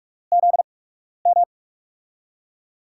Morse code, heard as a keyed tone, about 44 wpm, spaced as for 12 wpm, 700 Hz. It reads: ZM
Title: USA Call Signs - 6 Characters - 8x Character Spacing - 45wpm